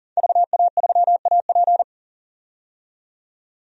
Morse code, heard as a keyed tone, about 40 wpm, 700 Hz. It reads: VA3AP
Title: DXCC-100 Call Signs - 5 Characters - 40wpm